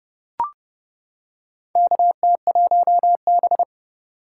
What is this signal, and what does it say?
Morse code, heard as a keyed tone, about 29 wpm, 700 Hz. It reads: KT1B
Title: K1USN SST QSO Practice - 30wpm